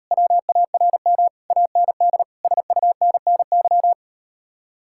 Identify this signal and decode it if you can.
Morse code, heard as a keyed tone, about 39 wpm, 700 Hz. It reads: WARM AND SUNNY